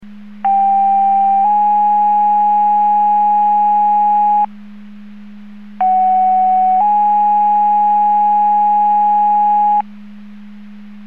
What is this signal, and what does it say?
Two-tone page, heard as a one-second tone, then a three-second tone, about 790 then 830 Hz, about 750 then 830 Hz. It is heard from 0.4 to 9.8 s.